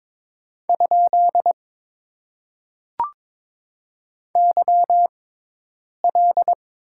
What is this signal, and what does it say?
Morse code, heard as a keyed tone, about 22 wpm, spaced as for 8 wpm, 700 Hz. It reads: ? YL
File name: Top 90 QSO Elements - 6x Character Spacing - 22wpm